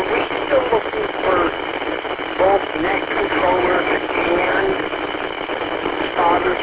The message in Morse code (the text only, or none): none